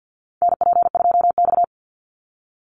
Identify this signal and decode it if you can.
Morse code, heard as a keyed tone, about 50 wpm, 700 Hz. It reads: NP2X
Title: Call Signs Repeated 2x - 50wpm